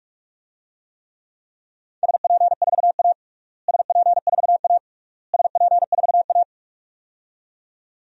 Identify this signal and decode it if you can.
Morse code, heard as a keyed tone, about 42 wpm, 700 Hz. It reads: SP4A SP4A SP4A
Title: DXCC-100 Call Signs - 4 Characters - Repeated 3x - 45wpm